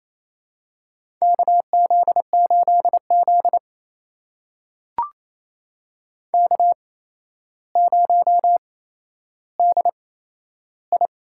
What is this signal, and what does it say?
Morse code, heard as a keyed tone, about 28 wpm, spaced as for 12 wpm, 700 Hz. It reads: KZ8Z K0DI
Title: USA Call Signs - 4 Characters - 8x Character Spacing - 28wpm